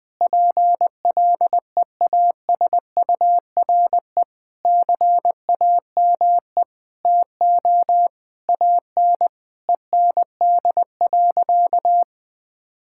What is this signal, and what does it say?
Morse code, heard as a keyed tone, about 20 wpm, 700 Hz. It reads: PLEASURE CAME TO AN END.
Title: Sentences from Top 1600 Words 20wpm